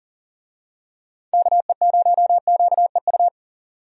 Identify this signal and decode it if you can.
Morse code, heard as a keyed tone, about 40 wpm, 700 Hz. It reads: KE0QEU
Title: USA Call Signs - 6 Characters - 40wpm